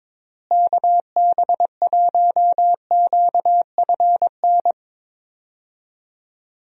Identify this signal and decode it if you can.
Morse code, heard as a keyed tone, about 22 wpm, 700 Hz. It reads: KB1QFN
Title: USA Call Signs - 6 Characters - Repeated 2x - 22wpm